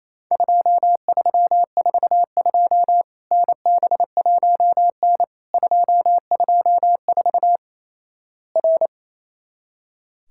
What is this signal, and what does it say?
Morse code, heard as a keyed tone, about 28 wpm, 700 Hz. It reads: 2342 NB1N 224 R
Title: Worked All Europe DX Contest - QTC Exchange - 28wpm